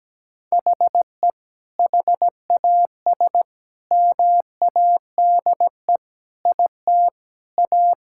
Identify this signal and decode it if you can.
Morse code, heard as a keyed tone, about 17 wpm, 700 Hz. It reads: HE HAS MADE IT A
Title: Sentences from Top 1000 Words - Review Part 6 - 17wpm